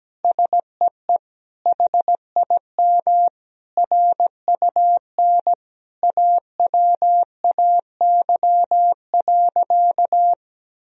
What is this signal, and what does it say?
Morse code, heard as a keyed tone, about 17 wpm, 700 Hz. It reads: SEE HIM RUN AWAY.
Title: Sentences from Top 1000 Words - Review Part 1 - 17wpm